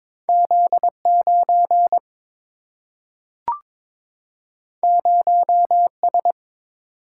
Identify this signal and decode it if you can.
Morse code, heard as a keyed tone, about 22 wpm, 700 Hz. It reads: Z 9 0 S
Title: Single-Letter Single-Number 22wpm